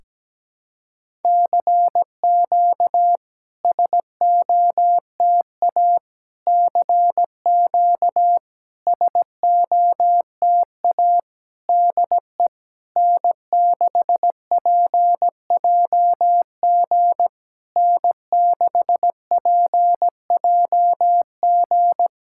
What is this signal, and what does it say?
Morse code, heard as a keyed tone, about 17 wpm, 700 Hz. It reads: CQ SOTA CQ SOTA DE N6PJG N6PJG
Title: SOTA QSO Practice 17wpm